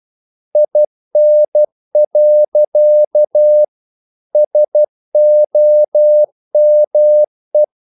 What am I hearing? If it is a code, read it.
Morse code, heard as a keyed tone, about 12 wpm, 600 Hz. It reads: IN. SOME
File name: CW / Practice / EnglishQuotations0112